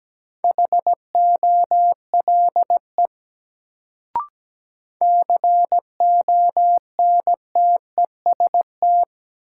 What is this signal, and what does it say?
Morse code, heard as a keyed tone, about 17 wpm, 700 Hz. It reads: HOLE CONTEST
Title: Sentences from Top 1600 Words - New Word Drill Alt - 17wpm